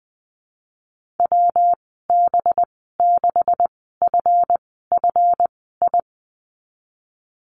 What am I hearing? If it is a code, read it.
Morse code, heard as a keyed tone, about 19 wpm, spaced as for 14 wpm, 700 Hz. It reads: WB6FFI
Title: 193 Lesson 31g - Letter v - Call Signs 20wpm